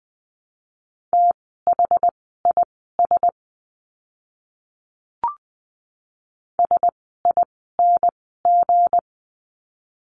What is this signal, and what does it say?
Morse code, heard as a keyed tone, about 20 wpm, spaced as for 15 wpm, 700 Hz. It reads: THIS SING
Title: 160 Lesson 26f - Letter g - Words 20wpm